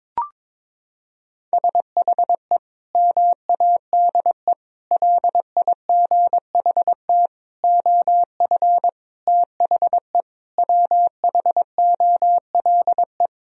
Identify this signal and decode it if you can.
Morse code, heard as a keyed tone, about 22 wpm, 700 Hz. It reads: SHE MADE LIGHT OF THE WHOLE
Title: Sentences from Top 1000 Words - Review Part 1 - 22wpm